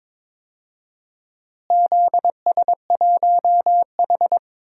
Morse code, heard as a keyed tone, about 22 wpm, 700 Hz. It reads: ZS1H
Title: DXCC-100 Call Signs - 4 Characters - Repeated 3x - 22wpm